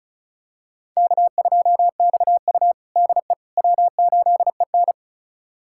Morse code, heard as a keyed tone, about 36 wpm, 700 Hz. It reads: K2XU DE W8EN
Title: USA Callsign DE Callsign - 4 Characters - 35wpm